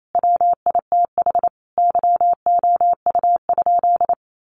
Morse code, heard as a keyed tone, about 28 wpm, 700 Hz. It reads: WITH YOU?
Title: Sentences from Top 1500 Words 28wpm